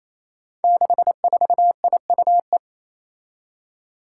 Morse code, heard as a keyed tone, about 29 wpm, 700 Hz. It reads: 64IUE
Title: Five Challenging Letter-Number Combinations 28wpm